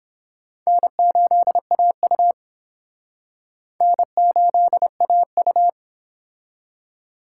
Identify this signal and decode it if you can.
Morse code, heard as a keyed tone, about 29 wpm, 700 Hz. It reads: N8AU N8AU